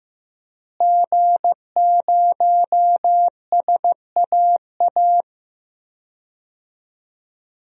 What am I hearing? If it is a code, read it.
Morse code, heard as a keyed tone, about 15 wpm, 700 Hz. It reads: G0SAA
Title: DXCC-100 Call Signs - 5 Characters - Repeated 2x - 15wpm